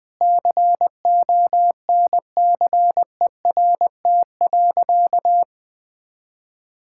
Morse code, heard as a keyed tone, about 20 wpm, 700 Hz. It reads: CONCERT.
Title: Sentences with 8-Word Phrases - 20wpm